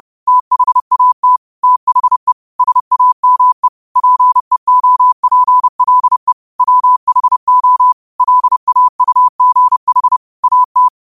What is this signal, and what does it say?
Morse code, heard as a keyed tone, about 30 wpm, 1000 Hz. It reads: THAT THE SAME PEOPLE WHO LAUGH AT